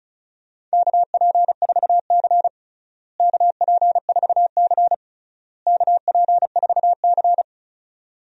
Morse code, heard as a keyed tone, about 36 wpm, 700 Hz. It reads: KP4C KP4C KP4C
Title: DXCC-100 Call Signs - 4 Characters - Repeated 3x - 35wpm